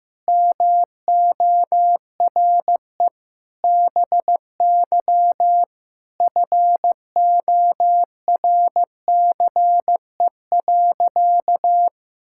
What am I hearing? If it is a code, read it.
Morse code, heard as a keyed tone, about 15 wpm, 700 Hz. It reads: MORE BY FORCE.
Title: Sentences from Top 1000 Words - Review Part 6 - 15wpm